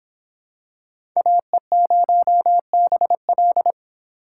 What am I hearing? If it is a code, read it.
Morse code, heard as a keyed tone, about 26 wpm, 700 Hz. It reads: AE0BL